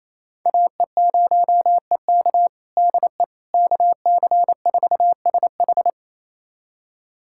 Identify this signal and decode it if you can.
Morse code, heard as a keyed tone, about 28 wpm, 700 Hz. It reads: AE0EK DE KC4SH
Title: USA Callsign DE Callsign - 5 Characters - 28wpm